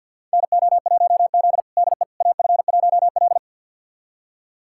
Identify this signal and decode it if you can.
Morse code, heard as a keyed tone, about 46 wpm, 700 Hz. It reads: NO1Z DE AF1L
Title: USA Callsign DE Callsign - 4 Characters - 50wpm